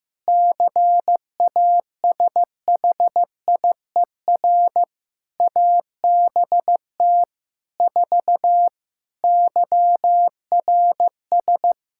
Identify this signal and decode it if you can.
Morse code, heard as a keyed tone, about 15 wpm, 700 Hz. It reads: CASHIER ABT 4 YRS
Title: Practice QSO - Exchange 3 - More Info - 15wpm